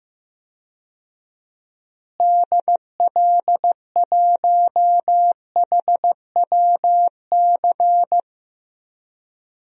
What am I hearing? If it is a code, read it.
Morse code, heard as a keyed tone, about 15 wpm, 700 Hz. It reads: DL1HWC
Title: DXCC-100 Call Signs - 6 Characters - Repeated 3x - 15wpm